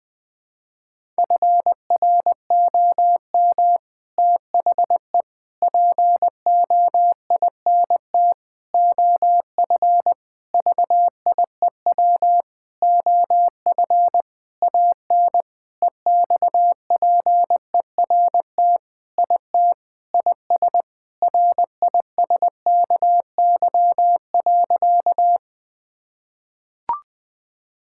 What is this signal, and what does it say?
Morse code, heard as a keyed tone, about 20 wpm, 700 Hz. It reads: FROM THE POINT OF VIEW OF AN EXPERT IT IS RISKY.
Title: Sentences with 6-Word Phrases - 20wpm